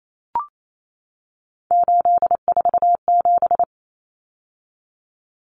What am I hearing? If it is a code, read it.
Morse code, heard as a keyed tone, about 28 wpm, 700 Hz. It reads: 847
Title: Numbers - 3-Digit 28wpm